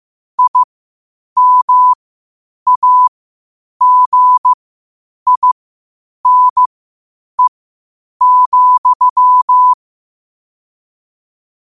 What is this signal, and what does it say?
Morse code, heard as a keyed tone, about 15 wpm, spaced as for 8 wpm, 1000 Hz. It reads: IMAGINE,
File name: qotdfiles / 8wpm